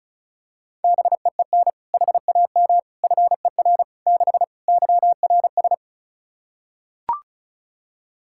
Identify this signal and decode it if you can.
Morse code, heard as a keyed tone, about 36 wpm, 700 Hz. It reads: BEEN HAM FER 6 YRS
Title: Practice QSO - Exchange 3 - More Info - 35wpm